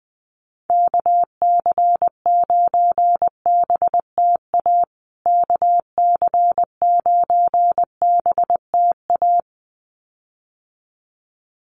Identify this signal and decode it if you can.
Morse code, heard as a keyed tone, about 20 wpm, 700 Hz. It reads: KC9BTA KC9BTA
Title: USA Call Signs Repeated 2x 20wpm